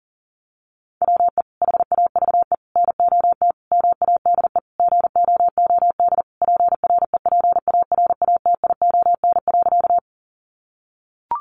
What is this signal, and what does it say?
Morse code, heard as a keyed tone, about 40 wpm, 700 Hz. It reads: WE HAVE NOT MADE GOOD PREPARATION.
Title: Sentences from Top 1400 Words 40wpm